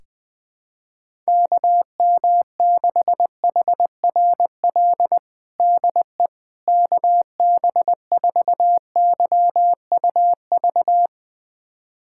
Morse code, heard as a keyed tone, about 20 wpm, 700 Hz. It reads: KM6HRL DE KB4YUV
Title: USA Callsign DE Callsign 20wpm